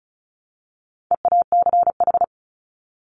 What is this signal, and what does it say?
Morse code, heard as a keyed tone, about 34 wpm, 700 Hz. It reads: EACH
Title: Top 200 Words 35wpm